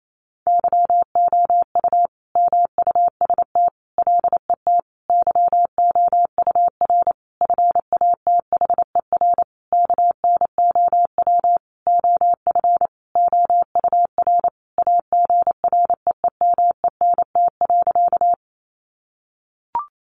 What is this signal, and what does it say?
Morse code, heard as a keyed tone, about 28 wpm, 700 Hz. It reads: YOU MUST LET YOUR FATHER KNOW OF OUR AGREEMENT.
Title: Sentences from Top 1500 Words 28wpm